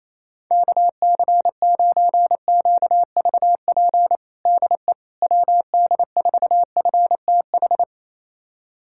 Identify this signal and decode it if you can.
Morse code, heard as a keyed tone, about 28 wpm, 700 Hz. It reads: KC9QVP DE WD4FTH
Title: USA Callsign DE Callsign - 6 Characters - 28wpm